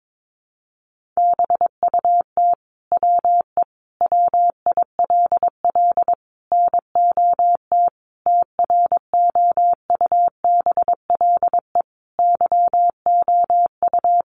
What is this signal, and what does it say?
Morse code, heard as a keyed tone, about 22 wpm, 700 Hz. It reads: BUT WE WILL NOT TROUBLE YOU
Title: Sentences from Top 1200 Words 22wpm